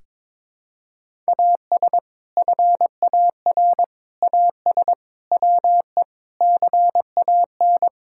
Morse code, heard as a keyed tone, about 21 wpm, 700 Hz. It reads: AS FAR AS WE CAN
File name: practice / 22wpm / Sets of 6 Words - 22wpm